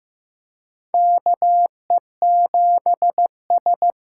Morse code, heard as a keyed tone, about 15 wpm, 700 Hz. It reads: KE7S
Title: USA Call Signs - 6 Characters - 8x Character Spacing - 15wpm